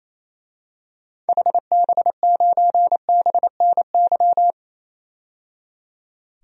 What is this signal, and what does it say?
Morse code, heard as a keyed tone, about 28 wpm, 700 Hz. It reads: HB9BNY
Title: DXCC-100 Call Signs - 6 Characters - 28wpm